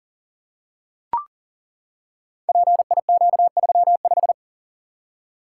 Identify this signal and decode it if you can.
Morse code, heard as a keyed tone, about 40 wpm, 690 Hz. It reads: PIQ35
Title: Code Groups - 5 Letter:Numbers - 40wpm